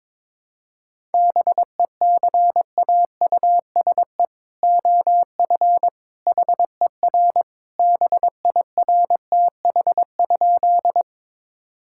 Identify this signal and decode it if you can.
Morse code, heard as a keyed tone, about 22 wpm, 700 Hz. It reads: BECAUSE OF HER BIRTH?
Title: Sentences from Top 1000 Words - Review Part 5 - 22wpm